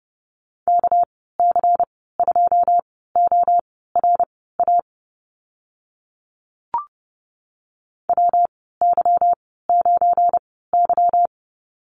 Morse code, heard as a keyed tone, about 30 wpm, spaced as for 17 wpm, 700 Hz. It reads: KC2ORA WY9Y
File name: Lesson 37g - Letter x - Call Signs 30wpm